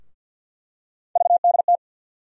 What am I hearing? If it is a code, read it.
Morse code, heard as a keyed tone, about 47 wpm, 700 Hz. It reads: VDT
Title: USA Call Sign Postfixes - 3 Letters 50wpm